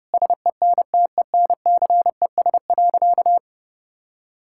Morse code, heard as a keyed tone, about 30 wpm, 700 Hz. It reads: SENTENCES.
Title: Pangram Challenge v2 - 30wpm